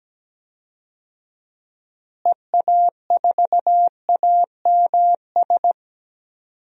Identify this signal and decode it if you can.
Morse code, heard as a keyed tone, about 17 wpm, 700 Hz. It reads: EA4AMS
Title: DXCC-100 Call Signs - 6 Characters - Repeated 3x - 17wpm